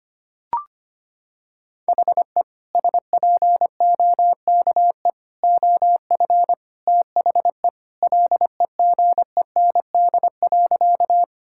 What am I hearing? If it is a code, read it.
Morse code, heard as a keyed tone, about 25 wpm, 700 Hz. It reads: HE SPOKE OF THE LEGEND.
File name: practice / 25wpm / Sentences with 2-Word Phrases - 25wpm